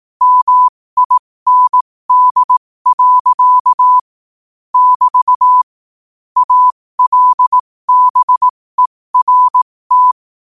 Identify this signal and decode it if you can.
Morse code, heard as a keyed tone, about 18 wpm, spaced as for 15 wpm, 1000 Hz. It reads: MIND. <BT> ALBERT
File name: qotdfiles / 15wpm